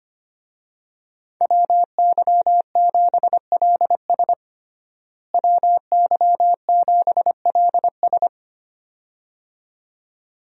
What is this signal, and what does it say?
Morse code, heard as a keyed tone, about 25 wpm, 700 Hz. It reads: WY7LS WY7LS
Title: USA Call Signs - 5 Characters - Repeated 2x - 25wpm